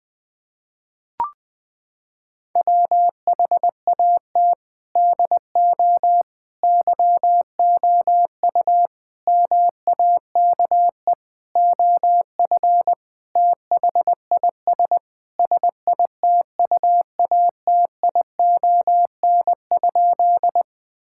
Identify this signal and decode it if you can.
Morse code, heard as a keyed tone, about 21 wpm, 700 Hz. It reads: WHAT DO YOU MAKE OF THIS SITUATION?
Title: Sentences with 5-Word Phrases - 20wpm